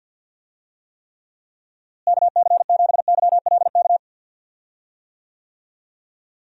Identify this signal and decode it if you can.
Morse code, heard as a keyed tone, about 52 wpm, 700 Hz. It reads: KC7YLK
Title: USA Call Signs - 6 Characters - Repeated 2x - 50wpm